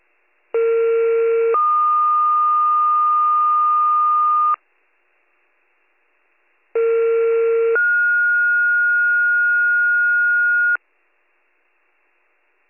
Two-tone page, heard as a one-second tone, then a three-second tone, about 460 then 1220 Hz, about 460 then 1430 Hz. It is heard from 0.5 to 10.8 s.